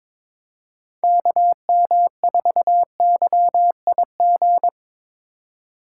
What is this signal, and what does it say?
Morse code, heard as a keyed tone, about 26 wpm, 700 Hz. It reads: KM4YIG
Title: Speed Racing - USA 2x3 Call Signs 22wpm